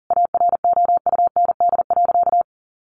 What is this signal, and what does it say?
Morse code, heard as a keyed tone, about 40 wpm, 700 Hz. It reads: AROUND.
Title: Easy Phrases 40wpm